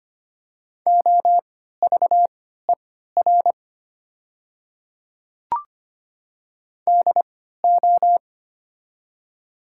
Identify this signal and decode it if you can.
Morse code, heard as a keyed tone, about 25 wpm, spaced as for 14 wpm, 690 Hz. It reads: OVER DO
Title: 3x Character Spacing - Top 100 Words - 25wpm